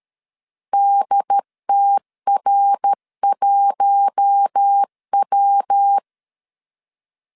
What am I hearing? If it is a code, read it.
Morse code, heard as a keyed tone, about 12 wpm, 800 Hz. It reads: DTR1W
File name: LEZIONI / 7 GF9Z / 7 GF9 12wpm 2319